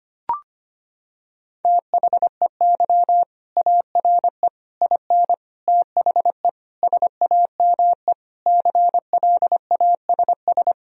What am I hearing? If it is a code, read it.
Morse code, heard as a keyed tone, about 25 wpm, 700 Hz. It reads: THEY ARE IN THE SAME CLASS
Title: Sentences from Top 1000 Words - Review Part 2 - 25wpm